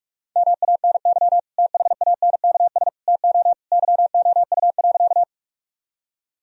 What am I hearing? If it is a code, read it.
Morse code, heard as a keyed tone, about 46 wpm, 700 Hz. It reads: MANY THANKS TO YOU.
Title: Sentences from Top 1000 Words - Review Part 3 - 45wpm